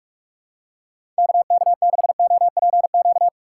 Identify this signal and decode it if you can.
Morse code, heard as a keyed tone, about 46 wpm, 700 Hz. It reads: KK6OPQ
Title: USA Call Signs - 6x Character Spacing - 45wpm